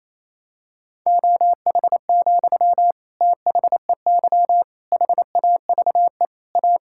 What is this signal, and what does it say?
Morse code, heard as a keyed tone, about 28 wpm, 700 Hz. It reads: OH, THEY HAVE A
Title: Sentences from Top 1000 Words - Review Part 5 - 28wpm